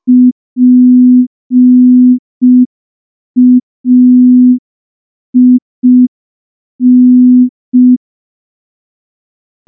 Morse code, heard as a keyed tone, about 5 wpm, 250 Hz. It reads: PAIN
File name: morse / morse-training-05WPM